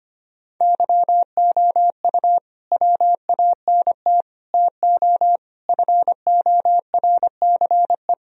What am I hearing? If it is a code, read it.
Morse code, heard as a keyed tone, about 25 wpm, 700 Hz. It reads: YOU WANT TO FORCE
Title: Sentences from Top 1000 Words - Review Part 3 - 25wpm